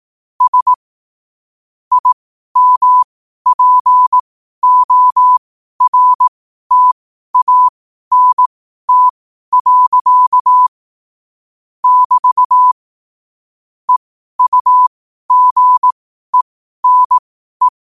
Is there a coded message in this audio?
Yes: Morse code from 0.4 to 17.7 s, heard as a keyed tone, about 18 wpm, spaced as for 12 wpm, 1000 Hz. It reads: S IMPORTANT. <BT> EUGENE